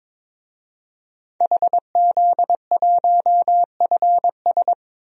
Morse code, heard as a keyed tone, about 22 wpm, 700 Hz. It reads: HZ1FS
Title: DXCC-100 Call Signs - 5 Characters - 22wpm